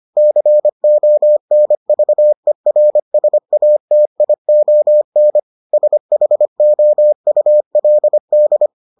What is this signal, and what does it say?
Morse code, heard as a keyed tone, about 25 wpm, 600 Hz. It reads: CONVERSATION SHOULD